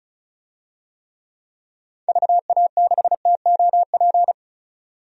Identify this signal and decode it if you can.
Morse code, heard as a keyed tone, about 36 wpm, 700 Hz. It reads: VA6TOP
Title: DXCC-100 Call Signs - 6 Characters - 35wpm